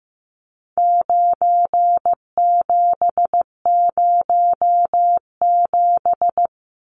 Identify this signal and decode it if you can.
Morse code, heard as a keyed tone, about 15 wpm, 700 Hz. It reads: 9707
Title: Numbers - 4-Digit 15wpm